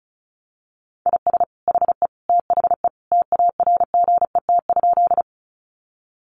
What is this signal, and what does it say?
Morse code, heard as a keyed tone, about 34 wpm, 700 Hz. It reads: IS HE THE TARGET?
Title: Sentences from Top 1200 Words 35wpm